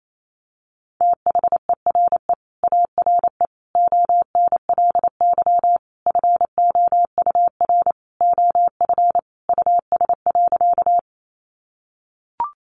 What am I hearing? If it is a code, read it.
Morse code, heard as a keyed tone, about 28 wpm, 700 Hz. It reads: THERE ARE ONLY FOUR OF US.
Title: Sentences from Top 300 Words 28wpm